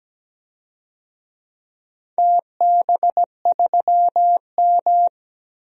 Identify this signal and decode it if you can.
Morse code, heard as a keyed tone, about 17 wpm, 700 Hz. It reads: TB3M
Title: DXCC-100 Call Signs - 4 Characters - 17wpm